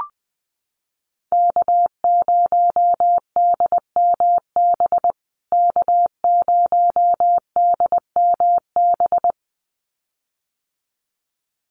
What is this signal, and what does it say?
Morse code, heard as a keyed tone, about 20 wpm, 700 Hz. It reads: K0DMB K0DMB
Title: USA Call Signs Repeated 2x 20wpm